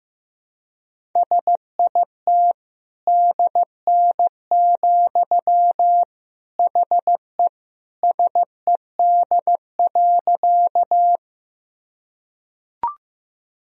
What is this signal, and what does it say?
Morse code, heard as a keyed tone, about 16 wpm, 700 Hz. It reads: SIT DN, HE SED.
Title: Sentences from Top 500 Words and Abbreviations 15wpm